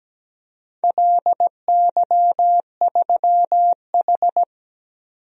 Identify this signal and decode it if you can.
Morse code, heard as a keyed tone, about 17 wpm, 700 Hz. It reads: LY3H
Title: DXCC-100 Call Signs - 4 Characters - 17wpm